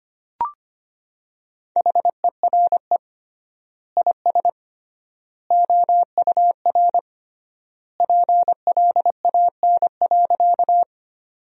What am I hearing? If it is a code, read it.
Morse code, heard as a keyed tone, about 25 wpm, 700 Hz. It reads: HERE IS OUR PLAN.
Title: 3x Word-Spacing - Sentences from Top 500 Words - 25wpm